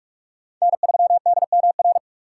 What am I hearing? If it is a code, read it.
Morse code, heard as a keyed tone, about 41 wpm, 700 Hz. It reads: N3DMR